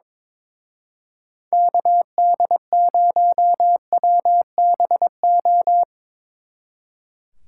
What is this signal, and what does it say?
Morse code, heard as a keyed tone, about 24 wpm, 700 Hz. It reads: KD0WBO
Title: Speed Racing - USA Call Signs 22wpm